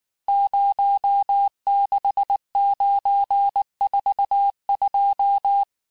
Morse code, heard as a keyed tone, about 19 wpm, 800 Hz. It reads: 06942
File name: cours-du-mercredi / audios / cours-869-texte-CHIFFRE-vitesse-18-MPM